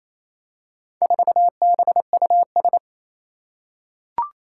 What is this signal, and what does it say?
Morse code, heard as a keyed tone, about 27 wpm, 700 Hz. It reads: 4BUS
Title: Four Challenging Letter-Number Combinations 28wpm